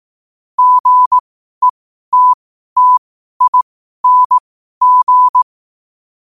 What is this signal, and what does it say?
Morse code, heard as a keyed tone, about 18 wpm, spaced as for 12 wpm, 1000 Hz. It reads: GETTING
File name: qotdfiles / 12wpm